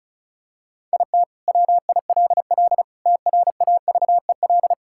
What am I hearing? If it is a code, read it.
Morse code, heard as a keyed tone, about 36 wpm, 700 Hz. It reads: IT WILL TRAVEL
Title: Sentences from Top 1000 Words - Review Part 6 - 35wpm